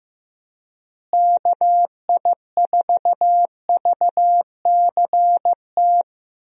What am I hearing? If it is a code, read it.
Morse code, heard as a keyed tone, about 15 wpm, 700 Hz. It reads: KI4VCT
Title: USA Call Signs - 6 Characters - Repeated 3x - 15wpm